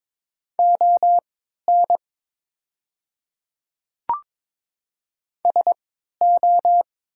Morse code, heard as a keyed tone, about 22 wpm, spaced as for 12 wpm, 700 Hz. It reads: ON SO
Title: 3x Character Spacing - Top 100 Words - 22wpm